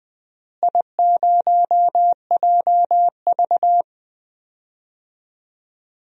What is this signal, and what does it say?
Morse code, heard as a keyed tone, about 20 wpm, 700 Hz. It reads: I0JV
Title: DXCC-100 Call Signs - 4 Characters - Repeated 2x - 20wpm